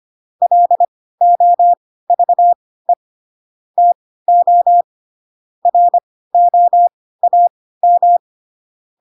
Morse code, heard as a keyed tone, about 25 wpm, spaced as for 16 wpm, 700 Hz. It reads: LOVE TO ROAM